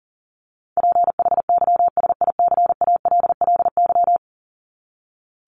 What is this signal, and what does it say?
Morse code, heard as a keyed tone, about 40 wpm, 700 Hz. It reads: PHYSICALLY
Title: Top 1501-1750 Words 40wpm